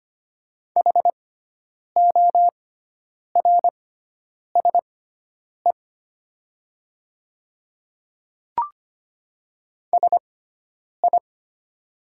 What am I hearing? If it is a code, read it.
Morse code, heard as a keyed tone, about 25 wpm, spaced as for 9 wpm, 700 Hz. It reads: HORSE SI
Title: 6x Character Spacing - Top 300 Words - 25wpm